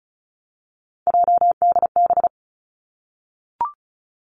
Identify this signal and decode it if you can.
Morse code, heard as a keyed tone, about 36 wpm, 700 Hz. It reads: JDB
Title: Three Challenging Letter-Number Combinations 35wpm